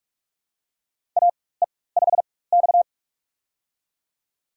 Morse code, heard as a keyed tone, about 46 wpm, spaced as for 21 wpm, 700 Hz. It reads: AE5X
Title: USA Call Signs - 4x Character Spacing - 45wpm